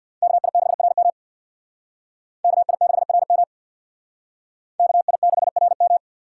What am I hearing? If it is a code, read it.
Morse code, heard as a keyed tone, about 60 wpm, 700 Hz. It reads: KI6RM KI6RM KI6RM
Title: Speed Racing - USA 2x2 Call Signs 50wpm